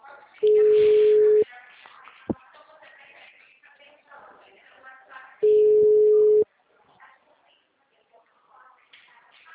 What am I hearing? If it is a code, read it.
Ringback tone (Europe, 425 Hz), with speech.
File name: out- -2012- - - .849